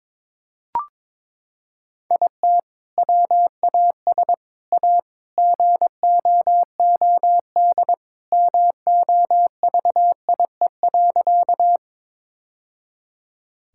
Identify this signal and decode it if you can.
Morse code, heard as a keyed tone, about 22 wpm, 700 Hz. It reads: IT WAS A GOOD MOVIE.
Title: Sentences from Top 1000 Words - Review Part 5 - 22wpm